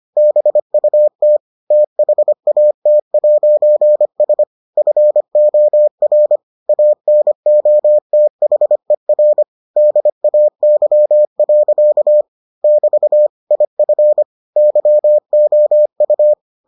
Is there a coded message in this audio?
Yes: Morse code from 0.2 to 16.3 s, heard as a keyed tone, about 25 wpm, 600 Hz. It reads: BUT THAT'S FOR ANOTHER DAY. <BT> IF YOU